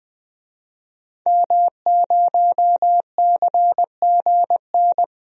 Morse code, heard as a keyed tone, about 20 wpm, 700 Hz. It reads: M0CGN
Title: DXCC-100 Call Signs - 5 Characters - Repeated 3x - 20wpm